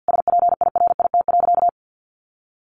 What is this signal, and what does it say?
Morse code, heard as a keyed tone, about 50 wpm, 700 Hz. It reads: SPIRIT.
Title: Sentences from Top 1400 Words 50wpm